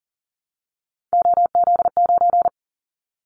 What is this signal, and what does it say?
Morse code, heard as a keyed tone, about 38 wpm, 700 Hz. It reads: OZ9
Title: Callsign Prefixes for Top 100 DXCC - 40wpm